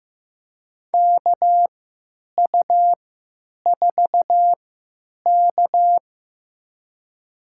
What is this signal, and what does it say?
Morse code, heard as a keyed tone, about 15 wpm, spaced as for 9 wpm, 700 Hz. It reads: KU4K
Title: USA Call Signs - 4 Characters - 3x Character Spacing - 15wpm